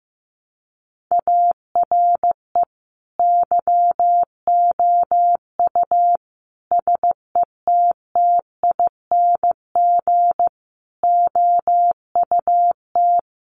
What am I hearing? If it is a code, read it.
Morse code, heard as a keyed tone, about 15 wpm, 700 Hz. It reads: ARE YOU SETTING OUT
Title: Sentences from Top 1300 Words 15wpm